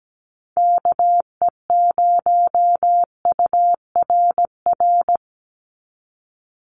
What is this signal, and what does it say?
Morse code, heard as a keyed tone, about 17 wpm, 700 Hz. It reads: KE0URR
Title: USA Call Signs 17wpm